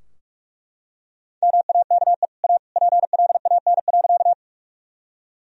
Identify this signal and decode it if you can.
Morse code, heard as a keyed tone, about 41 wpm, 700 Hz. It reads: MAKE A PLAN.
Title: Sentences from Top 1000 Words - Review Part 2 - 45wpm